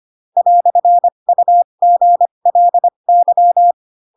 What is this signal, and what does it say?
Morse code, heard as a keyed tone, about 25 wpm, spaced as for 22 wpm, 700 Hz. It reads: "UGLY
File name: Pinocchio / Pinocchio ch3 18 25 ik1hge